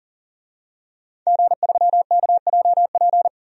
Morse code, heard as a keyed tone, about 40 wpm, 700 Hz. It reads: G3KJP
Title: DXCC-100 Call Signs - 5 Characters - 40wpm